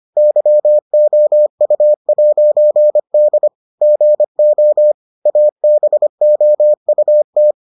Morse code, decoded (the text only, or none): YOU'D GO ABOUT